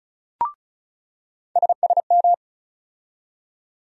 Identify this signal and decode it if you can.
Morse code, heard as a keyed tone, about 36 wpm, 700 Hz. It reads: SSM
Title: USA Call Sign Postfixes - 3 Letters 35wpm